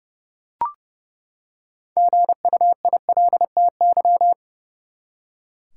Morse code, heard as a keyed tone, about 30 wpm, 700 Hz. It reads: GUILTY
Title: Six Letter Words - 30wpm